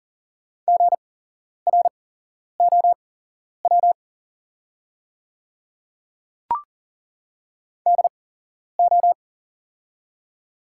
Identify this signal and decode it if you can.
Morse code, heard as a keyed tone, about 37 wpm, spaced as for 11 wpm, 700 Hz. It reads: GROW DO
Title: 8x Character Spacing - Top 300 Words - 40wpm